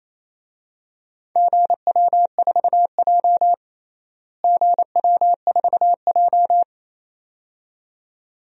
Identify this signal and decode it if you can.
Morse code, heard as a keyed tone, about 27 wpm, 700 Hz. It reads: GW4J GW4J
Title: DXCC-100 Call Signs - 4 Characters - Repeated 2x - 28wpm